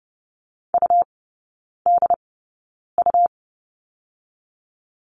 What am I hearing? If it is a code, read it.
Morse code, heard as a keyed tone, about 32 wpm, spaced as for 9 wpm, 700 Hz. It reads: UDU
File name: Lesson 18e - Letter m - Three Characters 30wpm